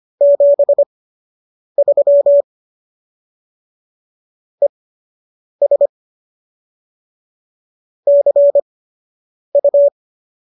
Morse code, heard as a keyed tone, about 25 wpm, spaced as for 8 wpm, 570 Hz. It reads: 73 ES CU